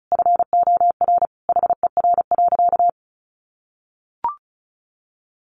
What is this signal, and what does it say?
Morse code, heard as a keyed tone, about 34 wpm, 700 Hz. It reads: FOR HER.
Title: Sentences from Top 900 Words 35wpm